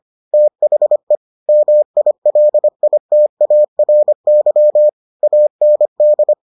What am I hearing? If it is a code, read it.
Morse code, heard as a keyed tone, about 25 wpm, 600 Hz. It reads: THE MILITARY AND